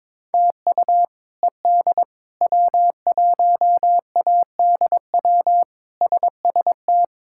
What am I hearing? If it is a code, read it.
Morse code, heard as a keyed tone, about 22 wpm, 700 Hz. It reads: TU ED W1ADW SST
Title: K1USN SST QSO Practice - 22wpm